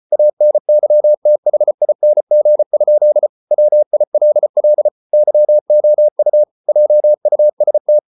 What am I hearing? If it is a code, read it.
Morse code, heard as a keyed tone, about 34 wpm, 600 Hz. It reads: ANYTHING? WILL YOU JUST